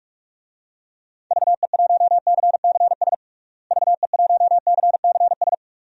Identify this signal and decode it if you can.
Morse code, heard as a keyed tone, about 46 wpm, 700 Hz. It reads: VE1CCS VE1CCS
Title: DXCC-100 Call Signs - 6 Characters - Repeated 2x - 45wpm